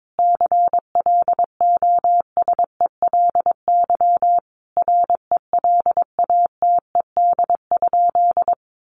Morse code, heard as a keyed tone, about 22 wpm, 700 Hz. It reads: CLOSELY RELATED?
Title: Sentences from Top 1400 Words 22wpm